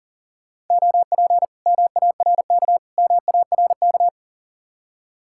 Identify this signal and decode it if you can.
Morse code, heard as a keyed tone, about 49 wpm, 700 Hz. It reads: OP MARK MARK
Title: Speed-Racing Common QSO Phrases Encore - 40wpm